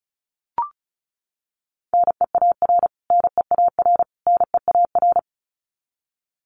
Morse code, heard as a keyed tone, about 34 wpm, 700 Hz. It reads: NEAR NEAR NEAR
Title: Top 500 Words Repeated 35wpm